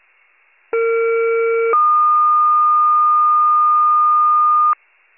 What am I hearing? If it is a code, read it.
Two-tone page, heard as a one-second tone, then a three-second tone, about 460 then 1220 Hz. It is heard from 0.7 to 4.7 s.